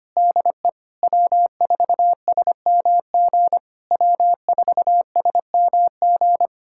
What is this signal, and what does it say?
Morse code, heard as a keyed tone, about 25 wpm, 700 Hz. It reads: DE W4SMG W4SMG
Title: SOTA QSO Practice 25wpm